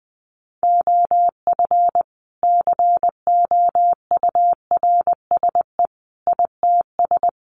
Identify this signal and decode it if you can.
Morse code, heard as a keyed tone, about 20 wpm, 700 Hz. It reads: OF COURSE ITS
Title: Sentences from Top 1200 Words 20wpm